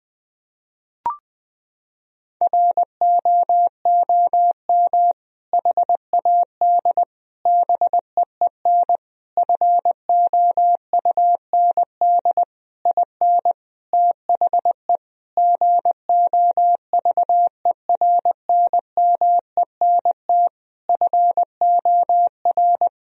Morse code, heard as a keyed tone, about 20 wpm, 700 Hz. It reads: ROOM HAD BEEN FOUND IN THE GOVERNMENT FOR
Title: Sentences from Top 1000 Words - Review Part 3 - 20wpm